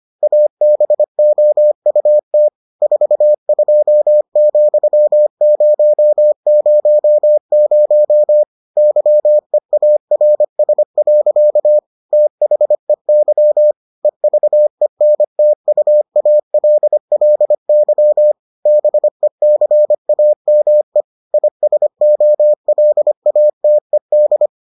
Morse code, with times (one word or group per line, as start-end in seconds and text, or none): 0.2-2.5 ABOUT
2.8-8.4 42,000
8.8-11.8 YEARS.
12.1-13.7 THEY
14.1-18.3 EVENTUALLY
18.7-21.0 BECAME
21.3-24.5 ISOLATED